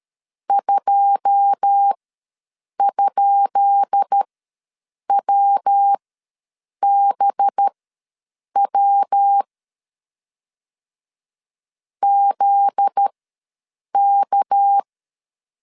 Morse code, heard as a keyed tone, about 13 wpm, spaced as for 7 wpm, 800 Hz. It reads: 2?WBW ZK
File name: LEZIONI / 10 SIGNS / 10B INT ws 12wpm 3527